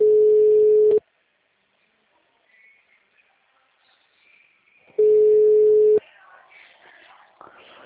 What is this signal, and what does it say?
Ringback tone (Europe, 425 Hz).